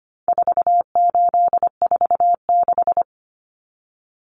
Morse code, heard as a keyed tone, about 25 wpm, 700 Hz. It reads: 4846
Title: Common Cut Numbers - 4-Digit 25wpm